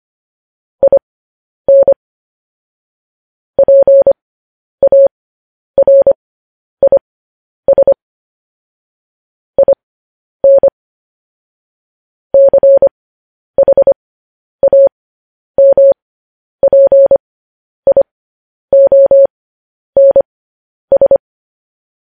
Morse code, heard as a keyed tone, about 25 wpm, spaced as for 10 wpm, 550 Hz. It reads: IN PARIS IN CHAMPIONS
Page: news headlines in morse code at 10 wpm